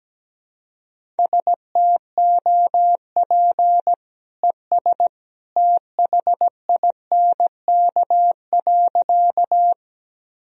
Morse code, heard as a keyed tone, about 17 wpm, 700 Hz. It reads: STOP ES THINK.
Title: Sentences from Top 300 Words and Abbreviations 17wpm